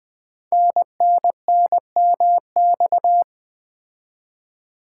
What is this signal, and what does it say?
Morse code, heard as a keyed tone, about 20 wpm, 700 Hz. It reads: NNNMX
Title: Code Groups - 5 Letters - 20wpm